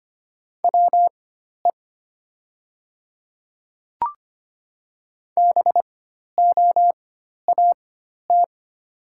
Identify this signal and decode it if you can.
Morse code, heard as a keyed tone, about 25 wpm, spaced as for 12 wpm, 690 Hz. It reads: WE BOAT
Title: 4x Character Spacing - Top 500 Words - 25wpm